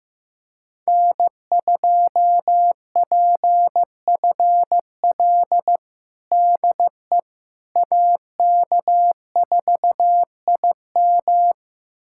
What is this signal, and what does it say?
Morse code, heard as a keyed tone, about 15 wpm, 700 Hz. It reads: N2PFL DE AK4IM
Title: USA Callsign DE Callsign - 5 Characters - 15wpm